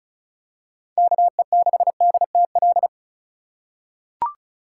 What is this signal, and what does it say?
Morse code, heard as a keyed tone, about 34 wpm, 700 Hz. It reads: KE6DTL
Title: USA Call Signs - 6 Characters - 4x Character Spacing - 35wpm